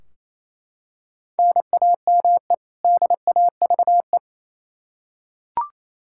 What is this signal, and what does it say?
Morse code, heard as a keyed tone, about 28 wpm, 700 Hz. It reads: NAME DAVE
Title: Practice QSO - Exchange 1 - Beginning - 28wpm